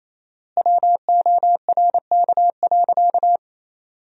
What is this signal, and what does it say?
Morse code, heard as a keyed tone, about 28 wpm, 700 Hz. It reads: WORK.
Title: 4x Word-Spacing - Sentences from Top 200 Words - 28wpm